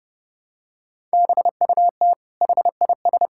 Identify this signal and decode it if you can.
Morse code, heard as a keyed tone, about 32 wpm, 700 Hz. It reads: BUT HIS
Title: Sentences from Top 1000 Words - Review Part 6 - 30wpm